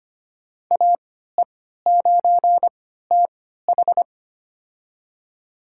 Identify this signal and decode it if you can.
Morse code, heard as a keyed tone, about 25 wpm, spaced as for 14 wpm, 700 Hz. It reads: AE9TH
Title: USA Call Signs - 5 Characters - 3x Character Spacing - 25wpm